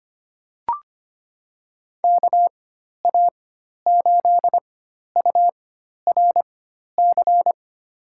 Morse code, heard as a keyed tone, about 25 wpm, spaced as for 12 wpm, 700 Hz. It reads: KA8URC
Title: USA Call Signs - 6 Characters - 4x Character Spacing - 25wpm